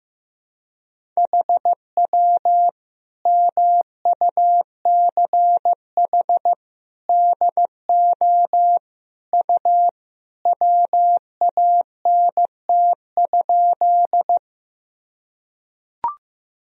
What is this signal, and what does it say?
Morse code, heard as a keyed tone, about 15 wpm, 700 Hz. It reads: HW MUCH DO U WANT?
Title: Sentences from Top 200 Words and Abbreviations 15wpm